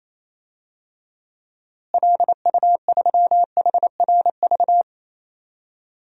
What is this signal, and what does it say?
Morse code, heard as a keyed tone, about 28 wpm, 700 Hz. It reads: LU3HRV
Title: DXCC-100 Call Signs - 6 Characters - Repeated 3x - 28wpm